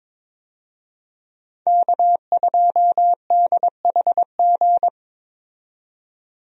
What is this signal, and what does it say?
Morse code, heard as a keyed tone, about 21 wpm, 700 Hz. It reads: K2DHG
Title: USA Call Signs - 5 Characters - 22wpm